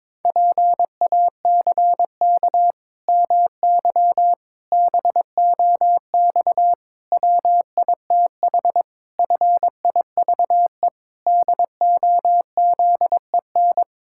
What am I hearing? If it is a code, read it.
Morse code, heard as a keyed tone, about 22 wpm, 700 Hz. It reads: PACK MY BOX WITH FIVE DOZEN
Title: Pangram Challenge v2 - 22wpm